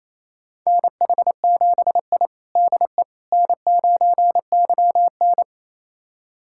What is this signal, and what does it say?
Morse code, heard as a keyed tone, about 27 wpm, 700 Hz. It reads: NH7I DE N9YN
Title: USA Callsign DE Callsign - 4 Characters - 28wpm